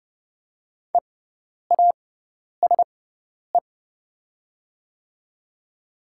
Morse code, heard as a keyed tone, about 30 wpm, spaced as for 10 wpm, 700 Hz. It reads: EASE
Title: 6x Character Spacing - Top 300 Words - 30wpm